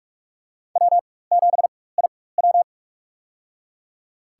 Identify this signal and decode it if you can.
Morse code, heard as a keyed tone, about 45 wpm, spaced as for 21 wpm, 700 Hz. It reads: W7IW
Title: USA Call Signs - 4 Characters - 4x Character Spacing - 45wpm